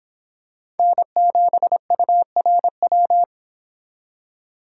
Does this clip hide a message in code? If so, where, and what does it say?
Yes: Morse code from 0.8 to 3.2 s, heard as a keyed tone, about 25 wpm, 700 Hz. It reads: N7URW